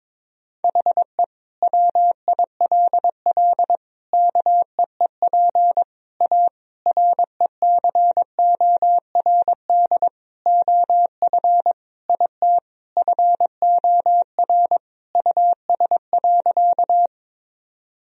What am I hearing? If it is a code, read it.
Morse code, heard as a keyed tone, about 22 wpm, 700 Hz. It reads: HE WILL KEEP A RECORD OF IT FOR US.
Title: Sentences from Top 1000 Words - Review Part 3 - 22wpm